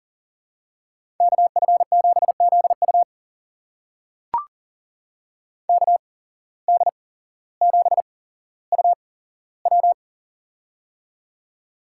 Morse code, heard as a keyed tone, about 40 wpm, spaced as for 17 wpm, 700 Hz. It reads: KF7ZU KD7UW
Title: USA Call Signs - 5 Characters - 8x Character Spacing - 40wpm